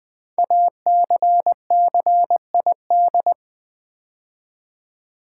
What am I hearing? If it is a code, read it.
Morse code, heard as a keyed tone, about 20 wpm, 700 Hz. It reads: ACCID
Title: Build A Word - 5 Letters At A Time - 12-Letter Words - 20wpm